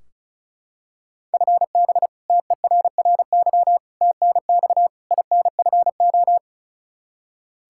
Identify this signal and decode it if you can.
Morse code, heard as a keyed tone, about 36 wpm, 700 Hz. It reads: FB TERRY TNX INFO
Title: Practice QSO - Exchange 3 - More Info - 35wpm